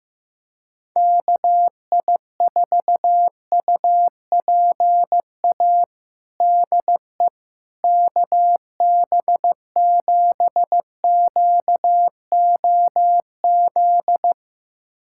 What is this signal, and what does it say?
Morse code, heard as a keyed tone, about 15 wpm, 700 Hz. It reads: KI4UPA DE KB7QOZ
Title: USA Callsign DE Callsign 15wpm